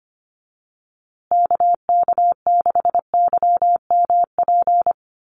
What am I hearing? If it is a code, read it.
Morse code, heard as a keyed tone, about 25 wpm, 700 Hz. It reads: KK6YMP
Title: USA Call Signs Repeated 3x 25wpm